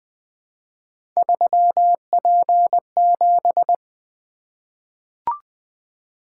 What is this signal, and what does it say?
Morse code, heard as a keyed tone, about 20 wpm, 700 Hz. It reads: 3P7
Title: Three Letter-Number Combinations - 20wpm